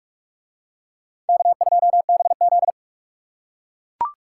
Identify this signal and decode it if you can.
Morse code, heard as a keyed tone, about 43 wpm, 700 Hz. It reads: K2BZ